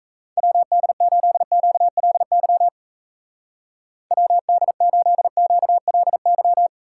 Morse code, heard as a keyed tone, about 35 wpm, 700 Hz. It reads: WD8QLY WD8QLY